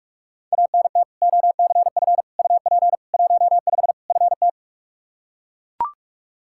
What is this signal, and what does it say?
Morse code, heard as a keyed tone, about 42 wpm, 700 Hz. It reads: ANT OCF UP 15 FT
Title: Practice QSO - Exchange 2 - Info - 45wpm